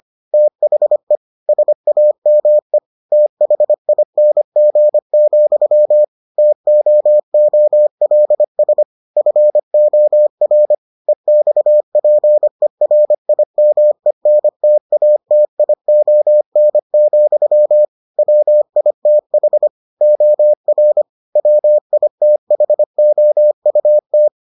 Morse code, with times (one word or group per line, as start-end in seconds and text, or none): 0.3-1.2 THE
1.5-2.8 SAME
3.1-6.0 THING,
6.4-8.8 TOOLS
9.2-10.7 FOR
11.1-17.9 EXPERIMENTATION,
18.2-19.7 WITH
20.0-21.0 OR
21.4-24.3 WITHOUT